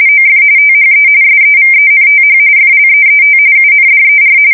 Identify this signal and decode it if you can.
RTTY, about 45.45 baud. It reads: OF THE SEVENTH FLEET UNDER
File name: fsk170 / USS-New-Jersey